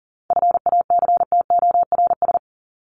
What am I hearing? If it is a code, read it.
Morse code, heard as a keyed tone, about 40 wpm, 700 Hz. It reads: FACTORS
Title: ES:S Suffix - Top 100 Words with Base Words 40wpm